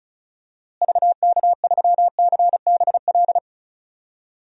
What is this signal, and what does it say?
Morse code, heard as a keyed tone, about 36 wpm, 700 Hz. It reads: VK3CBL
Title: DXCC-100 Call Signs - 6 Characters - Repeated 2x - 35wpm